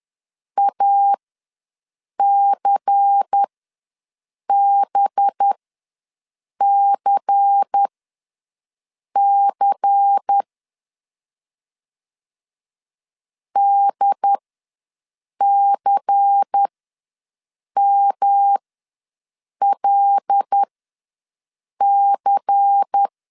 Morse code, heard as a keyed tone, about 10 wpm, spaced as for 6 wpm, 800 Hz. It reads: ACBCC DCMLC
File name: LEZIONI / 5 LBVC / 5 LBVC ws 10wpm 3733